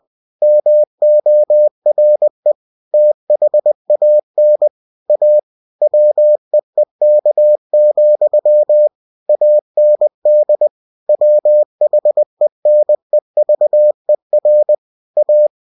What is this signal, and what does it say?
Morse code, heard as a keyed tone, about 20 wpm, 600 Hz. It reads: MORE THAN A WEEK, AND WHENEVER A